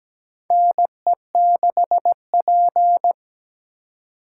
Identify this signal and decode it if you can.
Morse code, heard as a keyed tone, about 17 wpm, 700 Hz. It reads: NE6P
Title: USA Call Signs - 4 Characters - 4x Character Spacing - 17wpm